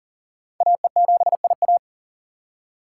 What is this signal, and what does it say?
Morse code, heard as a keyed tone, about 47 wpm, 700 Hz. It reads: AE7IA
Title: Speed Racing - USA 2x2 Call Signs 40wpm